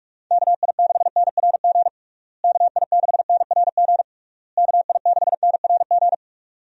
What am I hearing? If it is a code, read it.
Morse code, heard as a keyed tone, about 46 wpm, 700 Hz. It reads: KI6NRG KI6NRG KI6NRG
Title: USA Call Signs - 6 Characters - Repeated 3x - 45wpm